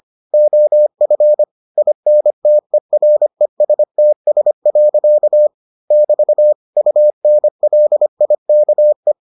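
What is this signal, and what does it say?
Morse code, heard as a keyed tone, about 25 wpm, 600 Hz. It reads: OF INTERESTS. <BT> UNLIKE
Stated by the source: Morse edition of the Foundations of Amateur Radio episodes.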